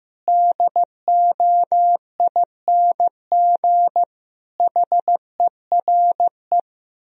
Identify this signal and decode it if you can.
Morse code, heard as a keyed tone, about 15 wpm, 700 Hz. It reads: DOING HERE
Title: Speed-Racing - Sets of 5 Words 15wpm